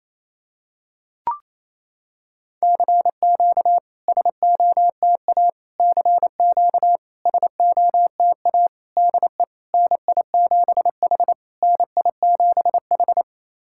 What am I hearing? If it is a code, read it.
Morse code, heard as a keyed tone, about 28 wpm, 700 Hz. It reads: CQ SOTA CQ SOTA DE NI7H NI7H
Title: SOTA QSO Practice 28wpm